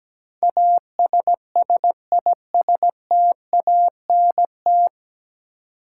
Morse code, heard as a keyed tone, about 17 wpm, 700 Hz. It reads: ASSISTANT
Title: Practice QSO - Exchange 3 - More Info - 17wpm